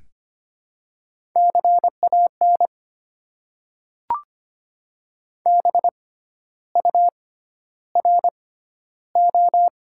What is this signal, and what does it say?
Morse code, heard as a keyed tone, about 25 wpm, spaced as for 12 wpm, 700 Hz. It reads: CAN BURO
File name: Top 90 QSO Elements - 6x Character Spacing - 25wpm